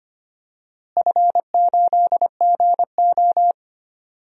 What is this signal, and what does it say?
Morse code, heard as a keyed tone, about 25 wpm, 700 Hz. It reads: F8GO
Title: DXCC-100 Call Signs - 4 Characters - Repeated 3x - 25wpm